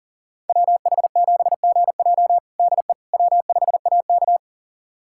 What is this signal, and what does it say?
Morse code, heard as a keyed tone, about 37 wpm, 700 Hz. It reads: WH7GJ DE W5AK
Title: USA Callsign DE Callsign 40wpm